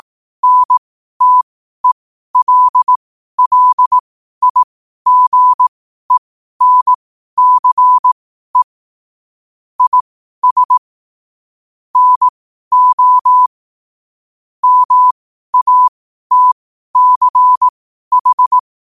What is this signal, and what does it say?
Morse code, heard as a keyed tone, about 18 wpm, spaced as for 12 wpm, 1000 Hz. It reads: NTELLIGENCE IS NO MATCH
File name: qotdfiles / 12wpm